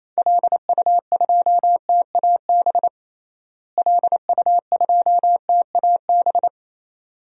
Morse code, heard as a keyed tone, about 28 wpm, 700 Hz. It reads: LU2TAB LU2TAB
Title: DXCC-100 Call Signs - 6 Characters - Repeated 2x - 28wpm